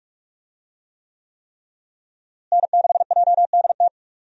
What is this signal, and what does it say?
Morse code, heard as a keyed tone, about 46 wpm, 700 Hz. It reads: N6JDT
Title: USA Call Signs - 5 Characters - Repeated 3x - 45wpm